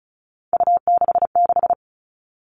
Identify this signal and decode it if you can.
Morse code, heard as a keyed tone, about 36 wpm, 700 Hz. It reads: U66
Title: Three Challenging Letter-Number Combinations 35wpm